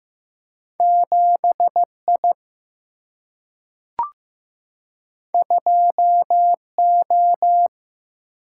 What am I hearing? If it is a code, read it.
Morse code, heard as a keyed tone, about 15 wpm, 700 Hz. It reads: 7I 2O
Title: Single-Letter Single-Number 15wpm